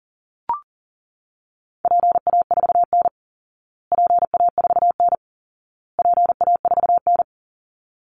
Morse code, heard as a keyed tone, about 40 wpm, 700 Hz. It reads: PA4N PA4N PA4N
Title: Call Signs Repeated 2x - 40wpm